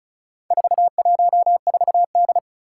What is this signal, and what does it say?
Morse code, heard as a keyed tone, about 36 wpm, 700 Hz. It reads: 414D
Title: Four Challenging Letter-Number Combinations 35wpm